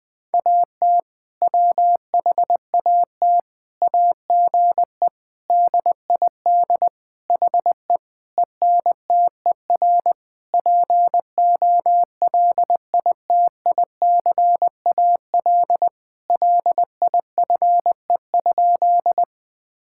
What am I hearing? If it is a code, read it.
Morse code, heard as a keyed tone, about 21 wpm, 700 Hz. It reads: AT WHAT AGE DID HE ENTER POLITICAL LIFE?
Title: Sentences from Top 1000 Words - Review Part 5 - 20wpm